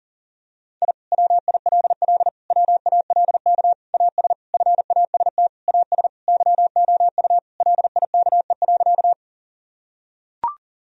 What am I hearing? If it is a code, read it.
Morse code, heard as a keyed tone, about 40 wpm, 700 Hz. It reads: I WILL WALK AS FAST AS YOU LIKE.
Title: Sentences from Top 1000 Words - Review Part 2 - 40wpm